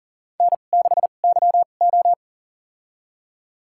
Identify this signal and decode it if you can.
Morse code, heard as a keyed tone, about 40 wpm, spaced as for 29 wpm, 700 Hz. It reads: N6YO
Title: USA Call Signs - 4 Characters - 2x Character Spacing - 40wpm